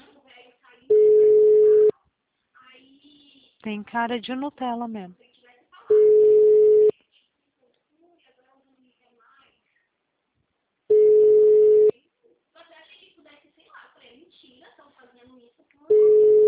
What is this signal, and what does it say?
Ringback tone (Europe, 425 Hz), with speech.
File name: out- -2019- - - .55324